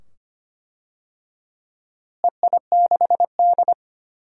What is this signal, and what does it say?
Morse code, heard as a keyed tone, about 25 wpm, 700 Hz. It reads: EI6D
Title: DXCC-100 Call Signs - 4 Characters - Repeated 2x - 25wpm